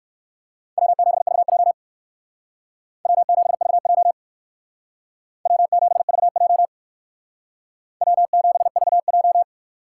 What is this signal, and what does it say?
Morse code, heard as a keyed tone, about 57 wpm, 700 Hz. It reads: W7VJ W7VJ W7VJ W7VJ